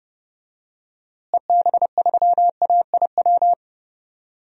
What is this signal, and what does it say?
Morse code, heard as a keyed tone, about 30 wpm, 700 Hz. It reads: EB3AIW
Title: DXCC-100 Call Signs - 6 Characters - Repeated 3x - 30wpm